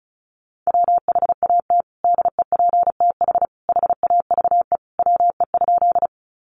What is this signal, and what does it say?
Morse code, heard as a keyed tone, about 34 wpm, 700 Hz. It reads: WHAT DEPTH HAVE WE?
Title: Sentences from Top 1200 Words 35wpm